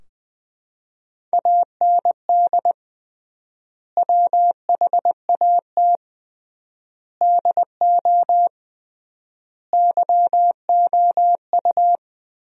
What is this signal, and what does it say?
Morse code, heard as a keyed tone, about 20 wpm, 700 Hz. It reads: AND WHAT DO YOU
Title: 3x Word-Spacing - Sentences from Top 100 Words - 20wpm